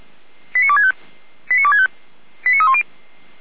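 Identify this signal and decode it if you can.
Five-tone selective call (ZVEI), heard as quick runs of five tones, tones of about 80 ms. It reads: 89265 89265 89310
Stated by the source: ZVEI five-tone call received on SDR, feuerwehr-orlamuende.org (fire brigade Orlamünde).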